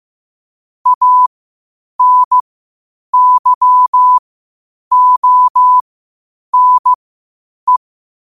Morse code, heard as a keyed tone, about 15 wpm, spaced as for 8 wpm, 1000 Hz. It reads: ANYONE
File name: qotdfiles / 8wpm